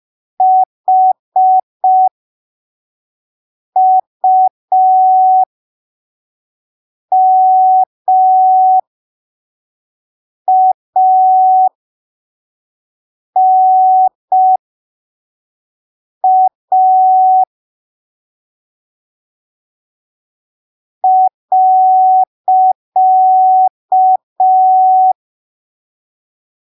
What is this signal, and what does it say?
Morse code, heard as a keyed tone, about 5 wpm, spaced as for 3 wpm, 750 Hz. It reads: HUMANA.
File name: arquivos / texto / texto 05 5ppm